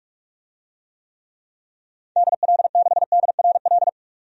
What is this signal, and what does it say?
Morse code, heard as a keyed tone, about 44 wpm, 700 Hz. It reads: DL6DRL
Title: DXCC-100 Call Signs - 6 Characters - 45wpm